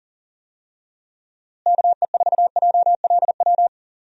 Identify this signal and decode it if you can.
Morse code, heard as a keyed tone, about 40 wpm, 700 Hz. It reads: KE4JLW
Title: USA Call Signs - 6 Characters - Repeated 2x - 40wpm